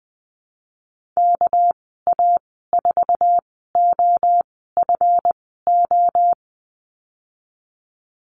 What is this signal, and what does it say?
Morse code, heard as a keyed tone, about 20 wpm, spaced as for 15 wpm, 700 Hz. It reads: KA4OFO
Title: 245 Lesson 39g - Letter z - Call Signs 20wpm